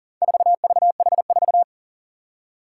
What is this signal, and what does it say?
Morse code, heard as a keyed tone, about 40 wpm, 700 Hz. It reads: 4VH4
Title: Four Challenging Letter-Number Combinations 40wpm